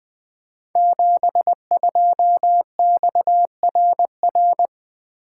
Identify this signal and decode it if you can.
Morse code, heard as a keyed tone, about 20 wpm, 700 Hz. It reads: 72XRR
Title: Code Groups - 5 Letter:Numbers - 20wpm